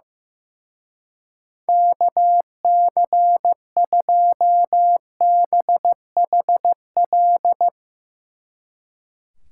Morse code, heard as a keyed tone, about 16 wpm, 700 Hz. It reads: KC2BHL
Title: Speed Racing - USA 2x3 Call Signs 15wpm